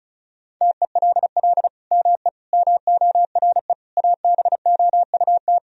Morse code, heard as a keyed tone, about 34 wpm, 700 Hz. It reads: TELL ME MORE ABOUT
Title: Sentences from Top 1000 Words - Review Part 4 - 35wpm